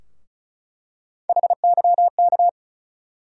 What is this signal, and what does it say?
Morse code, heard as a keyed tone, about 34 wpm, 700 Hz. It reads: HYK
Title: USA Call Sign Postfixes - 3 Letters 35wpm